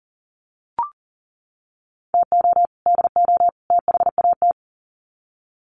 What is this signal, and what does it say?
Morse code, heard as a keyed tone, about 40 wpm, 700 Hz. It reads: TO DO THAT
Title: Sets of 3 Words 40wpm